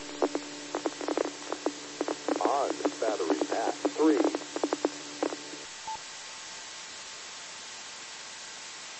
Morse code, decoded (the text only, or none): none